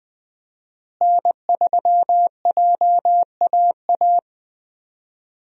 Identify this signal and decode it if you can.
Morse code, heard as a keyed tone, about 20 wpm, 700 Hz. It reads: N3JAA
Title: USA Call Signs - 5 Characters - Repeated 2x - 20wpm